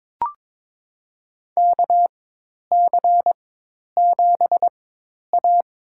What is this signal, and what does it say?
Morse code, heard as a keyed tone, about 21 wpm, spaced as for 10 wpm, 700 Hz. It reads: KC7A
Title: USA Call Signs - 4 Characters - 4x Character Spacing - 22wpm